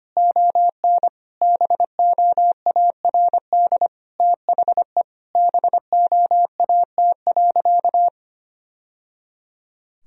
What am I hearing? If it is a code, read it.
Morse code, heard as a keyed tone, about 25 wpm, 700 Hz. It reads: ON BOARD THE BOAT.
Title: Sentences from Top 1000 Words - Review Part 4 - 25wpm